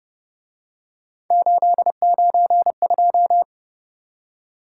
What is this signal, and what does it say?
Morse code, heard as a keyed tone, about 29 wpm, 700 Hz. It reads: 892